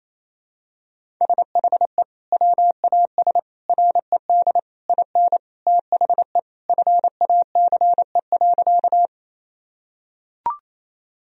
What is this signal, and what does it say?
Morse code, heard as a keyed tone, about 28 wpm, 700 Hz. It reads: SHE WAS RED IN THE FACE.
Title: Sentences from Top 1000 Words - Review Part 2 - 28wpm